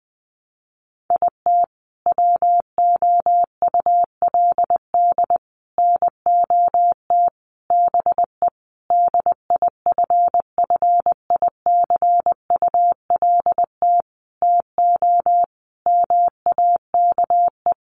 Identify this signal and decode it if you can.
Morse code, heard as a keyed tone, about 20 wpm, 700 Hz. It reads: IT WOULD NOT BE DIFFICULT TO MAKE
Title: Sentences from Top 600 Words 20wpm